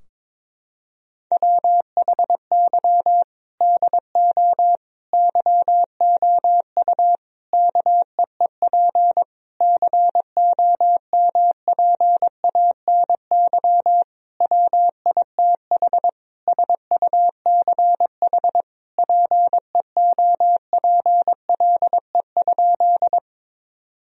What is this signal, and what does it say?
Morse code, heard as a keyed tone, about 22 wpm, 700 Hz. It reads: WHY DO YOU KEEP COMPANY WITH SUCH PEOPLE?
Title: Sentences from Top 1000 Words - Review Part 3 - 22wpm